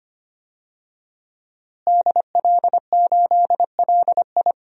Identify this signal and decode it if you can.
Morse code, heard as a keyed tone, about 25 wpm, 700 Hz. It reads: DL8LI
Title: DXCC-100 Call Signs - 5 Characters - Repeated 3x - 25wpm